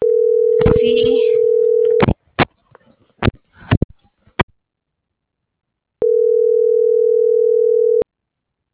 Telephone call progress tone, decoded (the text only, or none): ringback tone